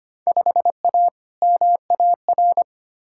Morse code, heard as a keyed tone, about 25 wpm, 700 Hz. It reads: 5A MAR
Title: Field Day Exchanges Encore - 25wpm